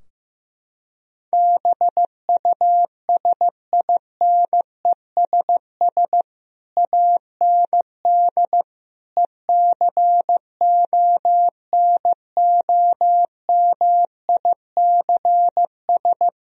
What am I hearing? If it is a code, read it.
Morse code, heard as a keyed tone, about 16 wpm, 700 Hz. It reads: BUSINESS AND ECONOMICS
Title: Sentences from Top 1000 Words - Review Part 3 - 15wpm